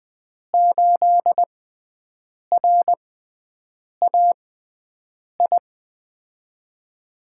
Morse code, heard as a keyed tone, about 20 wpm, spaced as for 7 wpm, 700 Hz. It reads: 8RAI
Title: USA Call Signs - 5 Characters - 6x Character Spacing - 20wpm